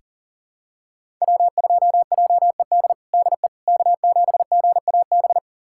Morse code, heard as a keyed tone, about 40 wpm, 700 Hz. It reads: W2JED DE K7GAB
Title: USA Callsign DE Callsign - 5 Characters - 40wpm